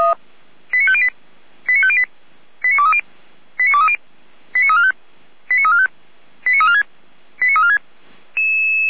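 Five-tone selective call (ZVEI), heard as quick runs of five tones, tones of about 70 ms. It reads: 89508 89508 89230 89230 89345 89345 89346 89346
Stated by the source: ZVEI five-tone selective call, address 89305, received on an SDR by Feuerwehr Orlamünde.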